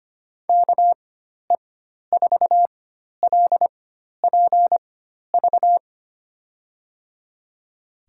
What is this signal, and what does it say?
Morse code, heard as a keyed tone, about 25 wpm, spaced as for 12 wpm, 700 Hz. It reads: KE4LPV
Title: USA Call Signs - 6 Characters - 4x Character Spacing - 25wpm